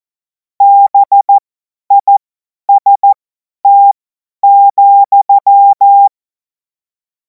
Morse code, heard as a keyed tone, about 14 wpm, spaced as for 10 wpm, 800 Hz. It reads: BIST,
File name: Morsekurs / Witze / Witz 21. April 2026 50 F70